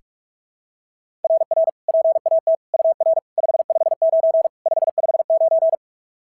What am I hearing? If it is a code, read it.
Morse code, heard as a keyed tone, about 44 wpm, 690 Hz. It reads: RR PAT UR 559 559
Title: SOTA QSO Practice 45wpm